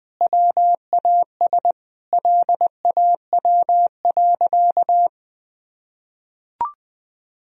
Morse code, heard as a keyed tone, about 21 wpm, 700 Hz. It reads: WAS LAW.
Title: Sentences from Top 1000 Words - Review Part 3 - 20wpm